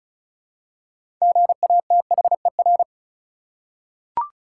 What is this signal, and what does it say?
Morse code, heard as a keyed tone, about 36 wpm, 700 Hz. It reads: GATHER
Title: Six Letter Words - 35wpm